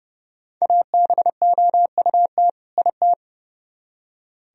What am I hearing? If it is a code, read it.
Morse code, heard as a keyed tone, about 36 wpm, 700 Hz. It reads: ABOUT IT
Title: Speed-Racing - Sets of 2 Words 30wpm